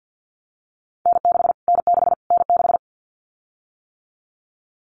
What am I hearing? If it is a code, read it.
Morse code, heard as a keyed tone, about 50 wpm, 700 Hz. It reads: N6 N6 N6
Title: Callsign Prefixes for Top 25 DXCC - Repeated 3x - 50wpm